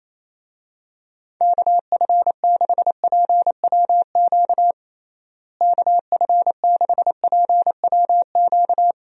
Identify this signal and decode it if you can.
Morse code, heard as a keyed tone, about 27 wpm, 700 Hz. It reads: KF6PWQ KF6PWQ
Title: USA Call Signs - 6 Characters - Repeated 3x - 28wpm